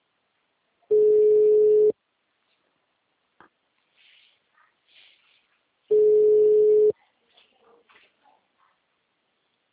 Ringback tone (Europe, 425 Hz).